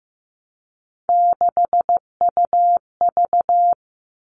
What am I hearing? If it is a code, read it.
Morse code, heard as a keyed tone, about 15 wpm, 700 Hz. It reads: 6UV
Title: All Cut Numbers - 3-Digit 15wpm